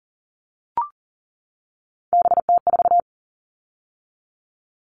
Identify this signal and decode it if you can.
Morse code, heard as a keyed tone, about 37 wpm, 700 Hz. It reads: BT4
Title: All Cut Numbers - 3-Digit 40wpm